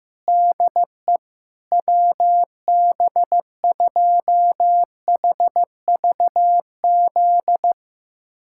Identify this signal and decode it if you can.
Morse code, heard as a keyed tone, about 15 wpm, 700 Hz. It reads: DE WB2HVZ
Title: USA Callsign DE Callsign 15wpm